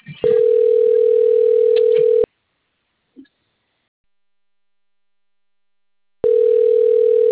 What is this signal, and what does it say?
Ringback tone (North America), 440 + 480 Hz.